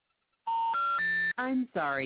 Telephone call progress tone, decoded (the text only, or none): special information tone